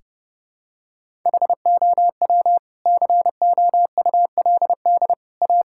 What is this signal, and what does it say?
Morse code, heard as a keyed tone, about 30 wpm, 700 Hz. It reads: HOW COULD A
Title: Sentences from Top 1000 Words - Review Part 6 - 30wpm